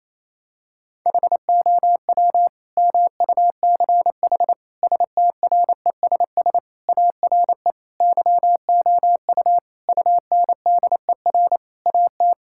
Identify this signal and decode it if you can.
Morse code, heard as a keyed tone, about 28 wpm, 700 Hz. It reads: HOW MUCH STRESS ARE YOU UNDER AT
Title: Sentences from Top 1000 Words - Review Part 6 - 28wpm